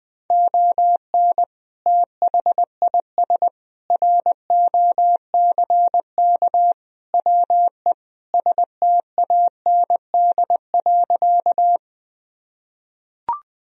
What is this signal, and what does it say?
Morse code, heard as a keyed tone, about 20 wpm, 700 Hz. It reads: ON THIS ROCK WE STAND.
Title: Sentences from Top 1000 Words - Review Part 2 - 20wpm